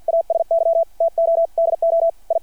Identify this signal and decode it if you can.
Morse code, heard as a keyed tone, about 53 wpm, 660 Hz. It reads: ASY TO DO I
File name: archived / CRSnet